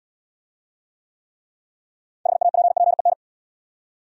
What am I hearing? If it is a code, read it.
Morse code, heard as a keyed tone, about 69 wpm, 700 Hz. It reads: HAPPEN